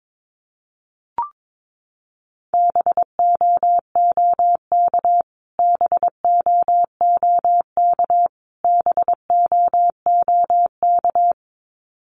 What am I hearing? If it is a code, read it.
Morse code, heard as a keyed tone, about 22 wpm, 700 Hz. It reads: BOOK BOOK BOOK
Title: Top 500 Words Repeated 22wpm